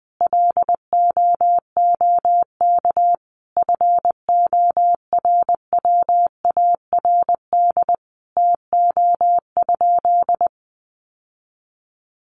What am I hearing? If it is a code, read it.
Morse code, heard as a keyed tone, about 20 wpm, 700 Hz. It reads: LOOK FORWARD TO?
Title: Sentences from Top 1300 Words 20wpm